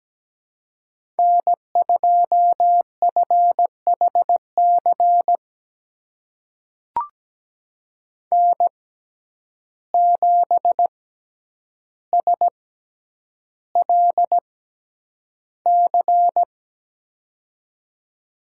Morse code, heard as a keyed tone, about 17 wpm, spaced as for 9 wpm, 700 Hz. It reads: N2FHC N7SLC
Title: USA Call Signs - 5 Characters - 6x Character Spacing - 17wpm